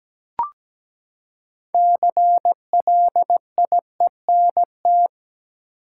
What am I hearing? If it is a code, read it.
Morse code, heard as a keyed tone, about 17 wpm, 700 Hz. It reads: CLIENT
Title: Six Letter Words - 17wpm